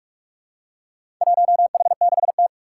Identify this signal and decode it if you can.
Morse code, heard as a keyed tone, about 44 wpm, 700 Hz. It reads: 1H6T
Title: Four Challenging Letter-Number Combinations 45wpm